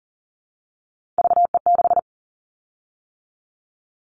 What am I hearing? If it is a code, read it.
Morse code, heard as a keyed tone, about 40 wpm, 700 Hz. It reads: VE6
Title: Callsign Prefixes for Top 25 DXCC - 40wpm